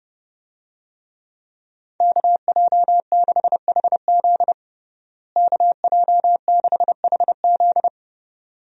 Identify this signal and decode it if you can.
Morse code, heard as a keyed tone, about 30 wpm, 700 Hz. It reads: KJ6HZ KJ6HZ
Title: USA Call Signs - 5 Characters - Repeated 2x - 30wpm